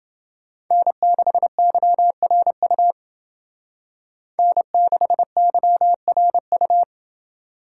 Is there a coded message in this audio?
Yes: Morse code from 0.7 to 6.8 s, heard as a keyed tone, about 28 wpm, 700 Hz. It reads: N6YRU N6YRU